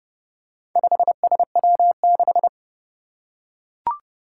Morse code, heard as a keyed tone, about 30 wpm, 690 Hz. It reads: 5SW6
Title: Four Challenging Letter-Number Combinations 30wpm